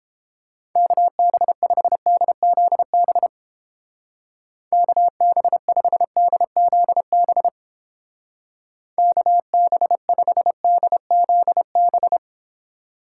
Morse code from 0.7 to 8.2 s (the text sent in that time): KB5DZB KB5DZB